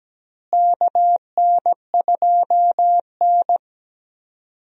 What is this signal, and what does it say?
Morse code, heard as a keyed tone, about 17 wpm, 700 Hz. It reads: KN2N
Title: USA Call Signs - 4 Characters - Repeated 3x - 17wpm